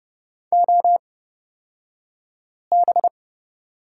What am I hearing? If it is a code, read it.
Morse code, heard as a keyed tone, about 30 wpm, spaced as for 5 wpm, 700 Hz. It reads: OB
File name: Lesson 38d - Letter q - Two Characters 30wpm